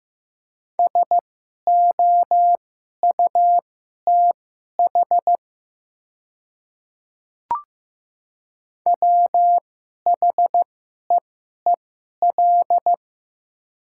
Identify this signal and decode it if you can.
Morse code, heard as a keyed tone, about 15 wpm, spaced as for 11 wpm, 690 Hz. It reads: SOUTH WHEEL
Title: 2x Character Spacing - Top 500 Words - 15wpm